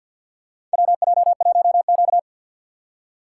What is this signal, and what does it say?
Morse code, heard as a keyed tone, about 48 wpm, 700 Hz. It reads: WJ1Q
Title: USA Call Signs - 2x Character Spacing - 50wpm